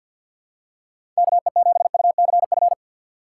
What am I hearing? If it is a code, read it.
Morse code, heard as a keyed tone, about 47 wpm, 700 Hz. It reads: KE7UCF
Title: USA Call Signs - 6 Characters - 3x Character Spacing - 50wpm